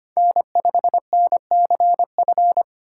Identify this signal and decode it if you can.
Morse code, heard as a keyed tone, about 25 wpm, 700 Hz. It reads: N5NCF
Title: USA Call Signs - 5 Characters - 4x Character Spacing - 25wpm